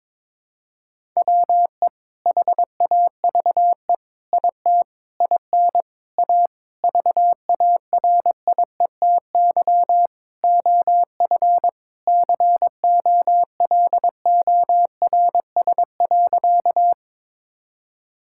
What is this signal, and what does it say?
Morse code, heard as a keyed tone, about 22 wpm, 700 Hz. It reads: WE HAVE IT IN A VARIETY OF COLORS.
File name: practice / 22wpm / Sentences with 4-Word Phrases - 22wpm